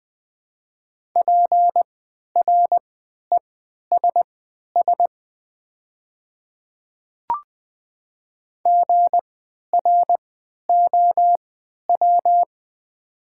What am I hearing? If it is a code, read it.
Morse code, heard as a keyed tone, about 20 wpm, spaced as for 11 wpm, 700 Hz. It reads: PRESS GROW
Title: 3x Character Spacing - Top 300 Words - 20wpm